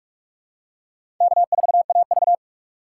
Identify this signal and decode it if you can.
Morse code, heard as a keyed tone, about 46 wpm, 700 Hz. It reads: K4AV
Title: USA Call Signs - 4 Characters - 3x Character Spacing - 45wpm